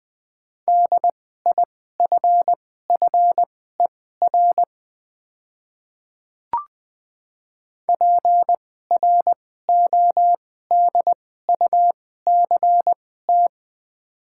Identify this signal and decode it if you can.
Morse code, heard as a keyed tone, about 20 wpm, spaced as for 14 wpm, 700 Hz. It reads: DIFFER PRODUCT
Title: 2x Character Spacing - Top 500 Words - 20wpm